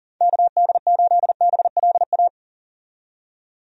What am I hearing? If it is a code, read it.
Morse code, heard as a keyed tone, about 40 wpm, 700 Hz. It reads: KD8BLA
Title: USA Call Signs - 6 Characters - 40wpm